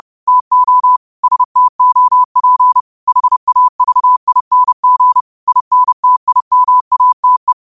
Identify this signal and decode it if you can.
Morse code, heard as a keyed tone, about 30 wpm, 1000 Hz. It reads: TO STOP HAVING INTIMATE